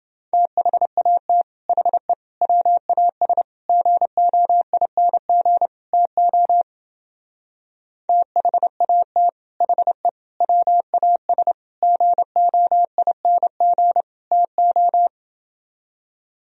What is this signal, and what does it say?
Morse code, heard as a keyed tone, about 28 wpm, 700 Hz. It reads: THAT HE WAS GOING TO THAT HE WAS GOING TO